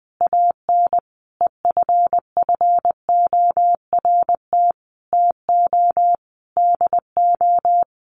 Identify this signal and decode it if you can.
Morse code, heard as a keyed tone, about 20 wpm, 700 Hz. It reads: AN EFFORT TO DO
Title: Sentences from Top 1200 Words 20wpm